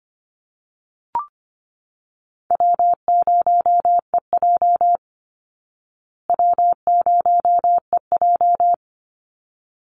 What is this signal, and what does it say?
Morse code, heard as a keyed tone, about 25 wpm, 700 Hz. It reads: W0EJ W0EJ
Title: Easy Call Signs Repeated 2x - 25wpm